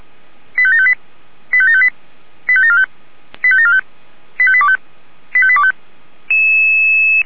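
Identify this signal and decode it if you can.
Five-tone selective call (ZVEI), heard as quick runs of five tones, tones of about 70 ms. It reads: 85758 85758 85745 85745 85825 85825
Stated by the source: ZVEI five-tone call received by SDR.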